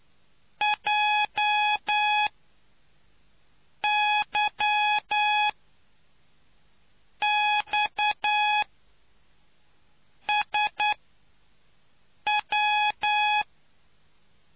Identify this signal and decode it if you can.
Morse code, heard as a keyed tone, about 9 wpm, spaced as for 4 wpm, 840 Hz. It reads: JYXSW